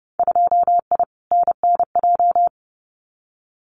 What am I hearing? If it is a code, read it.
Morse code, heard as a keyed tone, about 31 wpm, 700 Hz. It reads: 2I NNJ
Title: Winter Field Day Exchanges Encore 30wpm